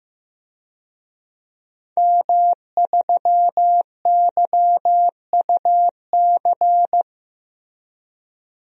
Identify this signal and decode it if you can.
Morse code, heard as a keyed tone, about 15 wpm, 700 Hz. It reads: M3YUC
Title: DXCC-100 Call Signs - 5 Characters - Repeated 3x - 15wpm